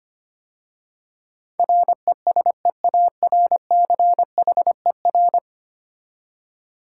Morse code, heard as a keyed tone, about 25 wpm, 700 Hz. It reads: RESEARCHER
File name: 10 Letter Words - First 3 Letters Given - 25wpm